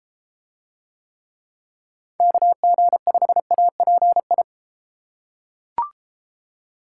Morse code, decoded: KG5API

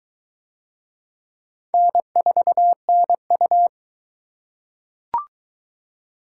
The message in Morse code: N4NU

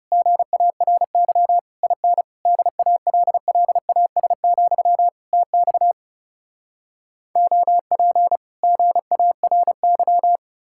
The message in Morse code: GARY IN DALLAS, TX OP GARY

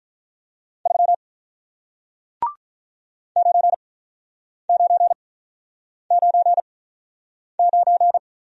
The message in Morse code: 3 9999

